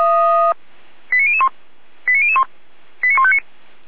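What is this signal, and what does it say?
Five-tone selective call (ZVEI), heard as quick runs of five tones, tones of about 70 ms. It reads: 89001 89001 89159
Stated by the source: Five-tone ZVEI selective call received on an SDR.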